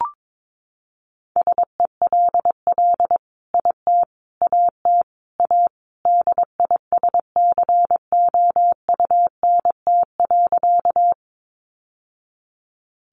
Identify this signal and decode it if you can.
Morse code, heard as a keyed tone, about 22 wpm, 700 Hz. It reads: SELL IT AT A DISCOUNT.
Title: Sentences from Top 1200 Words 22wpm